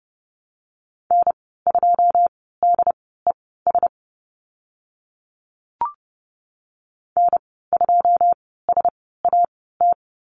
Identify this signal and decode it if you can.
Morse code, heard as a keyed tone, about 30 wpm, spaced as for 17 wpm, 700 Hz. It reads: N2DES N2SAT
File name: Lesson 15g - Number 5 - Call Signs 30wpm